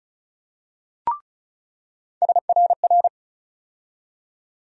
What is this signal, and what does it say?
Morse code, heard as a keyed tone, about 36 wpm, 700 Hz. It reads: SRR
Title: USA Call Sign Postfixes - 3 Letters 35wpm